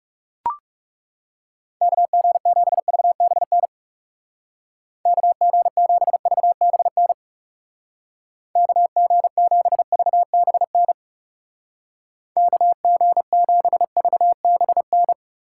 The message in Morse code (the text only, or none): KG7VBN KG7VBN KG7VBN KG7VBN